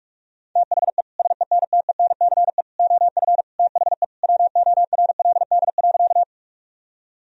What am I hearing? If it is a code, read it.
Morse code, heard as a keyed tone, about 46 wpm, 700 Hz. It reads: THE SENTENCE OF THE WORLD.
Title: Sentences from Top 1000 Words - Review Part 1 - 45wpm